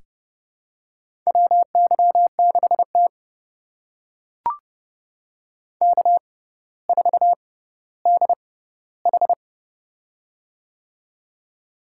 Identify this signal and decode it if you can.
Morse code, heard as a keyed tone, about 30 wpm, spaced as for 15 wpm, 700 Hz. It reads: WY6T K4DH
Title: USA Call Signs - 4 Characters - 6x Character Spacing - 30wpm